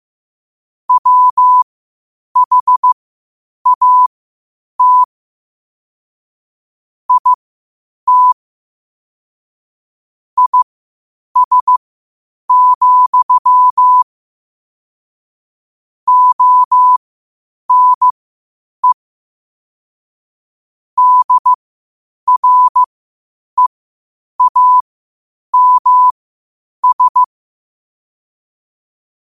Morse code, heard as a keyed tone, about 15 wpm, spaced as for 8 wpm, 1000 Hz. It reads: WHAT IT IS, ONE DREAMS
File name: qotdfiles / 8wpm